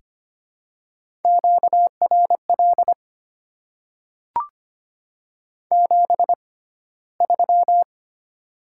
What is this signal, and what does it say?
Morse code, heard as a keyed tone, about 25 wpm, spaced as for 15 wpm, 690 Hz. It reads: QRL 73
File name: Top 20 QSO Elements - 6x Character Spacing - 25wpm